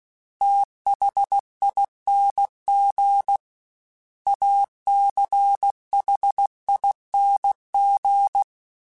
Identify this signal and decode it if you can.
Morse code, heard as a keyed tone, about 16 wpm, 800 Hz. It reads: THING ACHING
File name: Special INGs / Special INGs-mp3s / Sngl ing @15